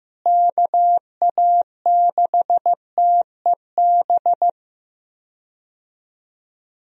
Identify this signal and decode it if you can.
Morse code, heard as a keyed tone, about 16 wpm, 700 Hz. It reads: KA6TEB
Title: USA Call Signs - 6 Characters - Repeated 3x - 15wpm